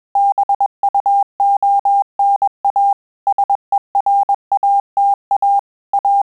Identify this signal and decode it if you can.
Morse code, heard as a keyed tone, about 21 wpm, 800 Hz. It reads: BUONA SERATA A
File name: Telegraphy-group / breve testo midi